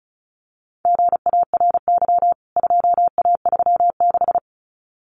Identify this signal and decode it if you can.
Morse code, heard as a keyed tone, about 34 wpm, 700 Hz. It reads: GARY 2A36
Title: CWT Exchange 35wpm